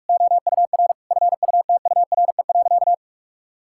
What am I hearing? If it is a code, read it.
Morse code, heard as a keyed tone, about 41 wpm, 700 Hz. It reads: OUR FUTURE.
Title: Sentences from Top 1000 Words - Review Part 6 - 45wpm